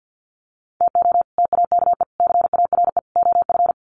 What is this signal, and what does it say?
Morse code, heard as a keyed tone, about 50 wpm, 700 Hz. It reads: TO TAKE CARE OF
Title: Sets of 4 Words 50wpm